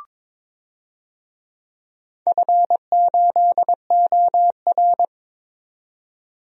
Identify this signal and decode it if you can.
Morse code, heard as a keyed tone, about 23 wpm, 700 Hz. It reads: F8OR
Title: DXCC-100 Call Signs - 4 Characters - 22wpm